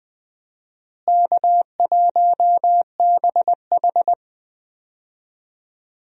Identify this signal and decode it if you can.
Morse code, heard as a keyed tone, about 21 wpm, 700 Hz. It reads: K1BH
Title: USA Call Signs - 4 Characters - Repeated 2x - 20wpm